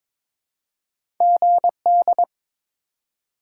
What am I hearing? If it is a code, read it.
Morse code, heard as a keyed tone, about 22 wpm, 700 Hz. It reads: GD
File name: Top 90 QSO Elements - 3x Character Spacing - 22wpm